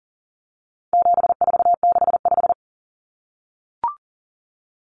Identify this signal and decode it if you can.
Morse code, heard as a keyed tone, about 40 wpm, 700 Hz. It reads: 7465
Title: Numbers - 4-Digit 40wpm